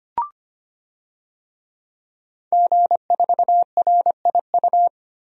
Morse code, heard as a keyed tone, about 25 wpm, 700 Hz. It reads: G4RIU
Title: DXCC-100 Call Signs - 5 Characters - 25wpm